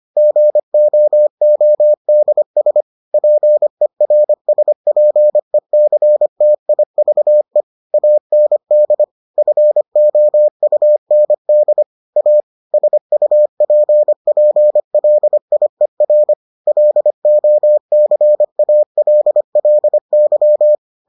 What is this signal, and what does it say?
Morse code, heard as a keyed tone, about 25 wpm, 600 Hz. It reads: GOODS PERSPECTIVE AND FOUND A SUPPLIER LOCALLY